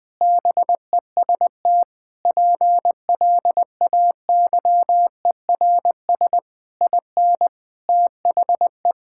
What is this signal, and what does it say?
Morse code, heard as a keyed tone, about 20 wpm, 700 Hz. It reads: BEST PLAYERS IN THE
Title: Sentences with 7-Word Phrases - Spoken Beginning - 20wpm